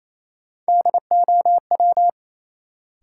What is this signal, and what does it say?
Morse code, heard as a keyed tone, about 28 wpm, 700 Hz. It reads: DOW
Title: Sets of 5 Words - New Word Drill - 28wpm